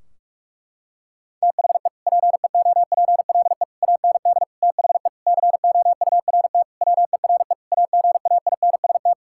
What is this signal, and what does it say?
Morse code, heard as a keyed tone, about 41 wpm, 700 Hz. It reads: THE PEOPLE AND THE COURT WERE AGAINST
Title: Sentences from Top 1600 Words 45wpm